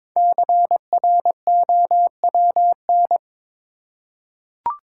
Morse code, heard as a keyed tone, about 22 wpm, 700 Hz. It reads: CROWN
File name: 5 Letter Words - First 3 Letters Given - 22wpm